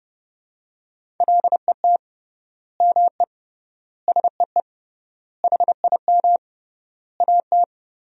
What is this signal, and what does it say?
Morse code, heard as a keyed tone, about 30 wpm, 700 Hz. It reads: LET ME SEE HIM AT
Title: 3x Word-Spacing - Sentences from Top 300 Words - 30wpm